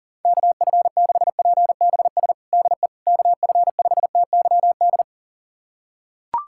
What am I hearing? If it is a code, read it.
Morse code, heard as a keyed tone, about 40 wpm, 700 Hz. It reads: KF6PBS DE KF5TYD
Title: USA Callsign DE Callsign - 6 Characters - 40wpm